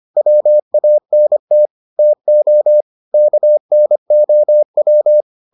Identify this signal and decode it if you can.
Morse code, heard as a keyed tone, about 25 wpm, 600 Hz. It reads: WANT TO KNOW